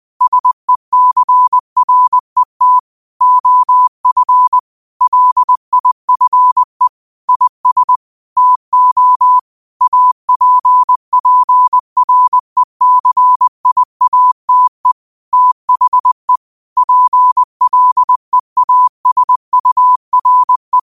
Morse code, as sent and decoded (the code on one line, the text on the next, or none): ... -.-. .-. - / --- ..-. / .-.. .. ..-. / .. ... / - --- / .- .--. .--. .-. -.-. .. .- - . / - .... / .--. .-.. .- ... ..- .-. .
SECRET OF LIFE IS TO APPRECIATE THE PLEASURE